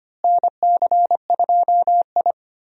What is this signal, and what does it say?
Morse code, heard as a keyed tone, about 25 wpm, 700 Hz. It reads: NC2I
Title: USA Call Signs - 4 Characters - Repeated 2x - 25wpm